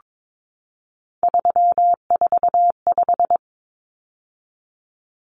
Morse code, heard as a keyed tone, about 22 wpm, 700 Hz. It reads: 345
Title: Numbers - 3-Digit 22wpm